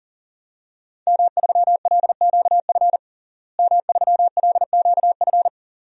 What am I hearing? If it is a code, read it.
Morse code, heard as a keyed tone, about 40 wpm, 700 Hz. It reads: M3LQF M3LQF
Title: DXCC-100 Call Signs - 5 Characters - Repeated 2x - 40wpm